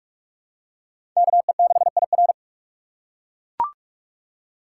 Morse code, heard as a keyed tone, about 42 wpm, 700 Hz. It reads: KE6IR
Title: USA Call Signs - 5 Characters - 3x Character Spacing - 45wpm